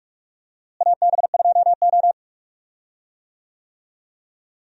Morse code, heard as a keyed tone, about 47 wpm, 700 Hz. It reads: AB2O
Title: USA Call Signs - 4 Characters - Repeated 3x - 45wpm